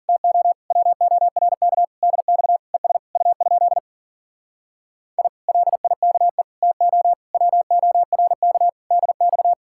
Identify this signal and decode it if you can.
Morse code, heard as a keyed tone, about 50 wpm, 700 Hz. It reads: TO WORK DX ES U? I LIKE TO WORK DX